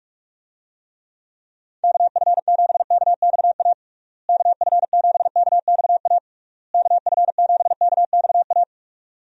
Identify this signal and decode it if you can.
Morse code, heard as a keyed tone, about 46 wpm, 700 Hz. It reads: KF7KXA KF7KXA KF7KXA
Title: USA Call Signs - 6 Characters - Repeated 3x - 45wpm